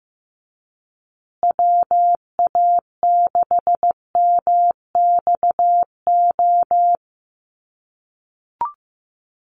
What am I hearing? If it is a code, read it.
Morse code, heard as a keyed tone, about 15 wpm, 700 Hz. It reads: WA6MXO
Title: USA Call Signs Repeated 2x 15wpm